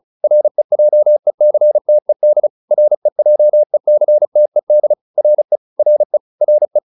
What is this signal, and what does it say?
Morse code, heard as a keyed tone, about 35 wpm, 600 Hz. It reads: REJECTED REJECTED RE RE RE